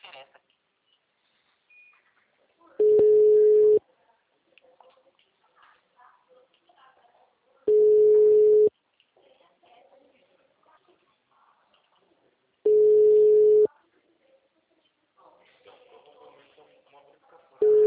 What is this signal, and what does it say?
Ringback tone (Europe, 425 Hz), with speech.